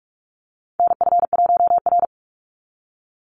Morse code, heard as a keyed tone, about 46 wpm, 690 Hz. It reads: NF1R
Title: NA SOTA Chaser Callsigns 45wpm